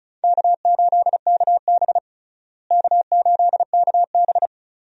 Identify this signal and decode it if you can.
Morse code, heard as a keyed tone, about 36 wpm, 700 Hz. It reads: K8KB K8KB
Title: USA Call Signs - 4 Characters - Repeated 3x - 35wpm